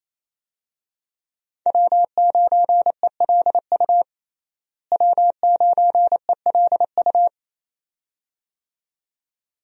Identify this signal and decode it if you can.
Morse code, heard as a keyed tone, about 28 wpm, 700 Hz. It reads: W9ELU W9ELU
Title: USA Call Signs - 5 Characters - Repeated 2x - 28wpm